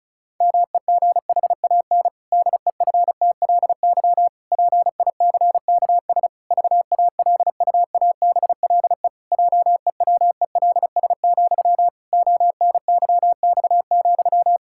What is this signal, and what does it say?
Morse code, heard as a keyed tone, about 36 wpm, 700 Hz. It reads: MEGHAN DEFTLY PICKS VALUABLE JEWELS, ONYX,
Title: Pangram Challenge v2 - 35wpm